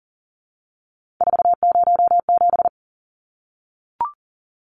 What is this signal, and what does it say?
Morse code, heard as a keyed tone, about 40 wpm, 700 Hz. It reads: 407
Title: Numbers - 3-Digit 40wpm